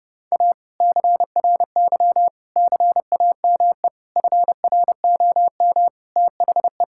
Morse code, heard as a keyed tone, about 32 wpm, 700 Hz. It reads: A CRY CAME FROM THE
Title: Sentences from Top 1000 Words - Review Part 5 - 30wpm